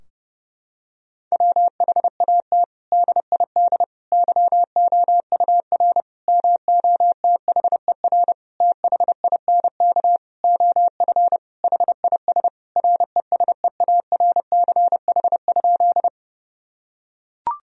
Morse code, heard as a keyed tone, about 30 wpm, 700 Hz. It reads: WHAT DID YOUR MOTHER THINK OF HIS RESEARCH?
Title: Sentences from Top 1000 Words - Review Part 4 - 30wpm